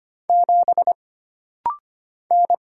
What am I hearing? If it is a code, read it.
Morse code, heard as a keyed tone, about 25 wpm, 700 Hz. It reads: 7 N
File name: Single Character (Letter:Number:Punctuation:Prosign) - Rapid-Fire - 25wpm